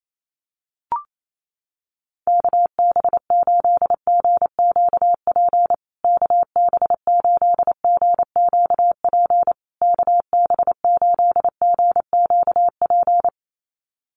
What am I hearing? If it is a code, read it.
Morse code, heard as a keyed tone, about 28 wpm, 700 Hz. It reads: KB8GQP KB8GQP KB8GQP
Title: USA Call Signs Repeated 3x 28wpm